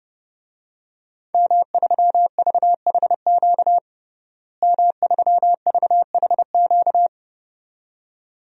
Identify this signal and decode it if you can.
Morse code, heard as a keyed tone, about 30 wpm, 700 Hz. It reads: M3VHQ M3VHQ
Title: DXCC-100 Call Signs - 5 Characters - Repeated 2x - 30wpm